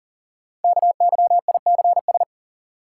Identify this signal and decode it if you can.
Morse code, heard as a keyed tone, about 39 wpm, 700 Hz. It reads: KYICS
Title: Code Groups - 5 Letters - 40wpm